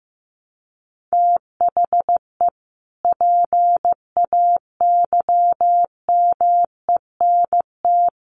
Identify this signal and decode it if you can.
Morse code, heard as a keyed tone, about 15 wpm, 700 Hz. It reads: THE PAYMENT
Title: Sentences from Top 1200 Words 15wpm